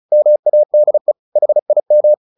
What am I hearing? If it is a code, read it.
Morse code, heard as a keyed tone, about 34 wpm, 600 Hz. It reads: MADE HIM
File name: the old man and the sea / the old man and the sea 1 35wpm